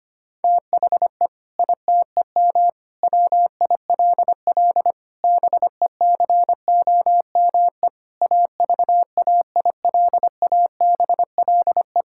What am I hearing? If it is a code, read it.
Morse code, heard as a keyed tone, about 25 wpm, 700 Hz. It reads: THE ITEM WILL BECOME AVAILABLE
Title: Sentences from Top 1000 Words - Review Part 6 - 25wpm